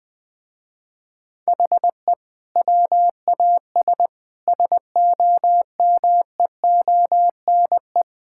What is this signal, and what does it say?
Morse code, heard as a keyed tone, about 21 wpm, 700 Hz. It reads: HE WAS SOMEONE
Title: Sentences with 6-Word Phrases - Spoken Beginning - 20wpm